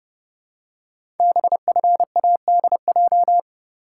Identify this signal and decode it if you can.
Morse code, heard as a keyed tone, about 28 wpm, 700 Hz. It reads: BFADJ
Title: Code Groups - 5 Letters - 30wpm